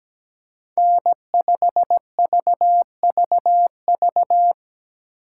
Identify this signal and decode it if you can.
Morse code, heard as a keyed tone, about 17 wpm, 700 Hz. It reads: N5VVV
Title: USA Call Signs - 5 Characters - 4x Character Spacing - 17wpm